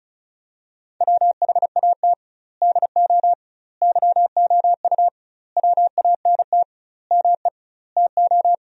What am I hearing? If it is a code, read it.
Morse code, heard as a keyed tone, about 34 wpm, 700 Hz. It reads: WHAT DO YOU WANT ME TO
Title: 2x Word-Spacing - Sentences from Top 200 Words - 35wpm